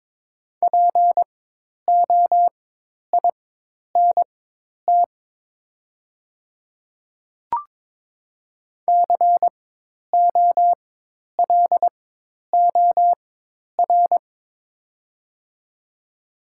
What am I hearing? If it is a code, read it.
Morse code, heard as a keyed tone, about 22 wpm, spaced as for 10 wpm, 700 Hz. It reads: POINT COLOR
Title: 4x Character Spacing - Top 500 Words - 22wpm